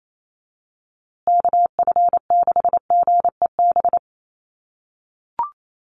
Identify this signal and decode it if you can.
Morse code, heard as a keyed tone, about 28 wpm, 700 Hz. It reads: KF6GEB
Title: USA Call Signs Repeated 3x 28wpm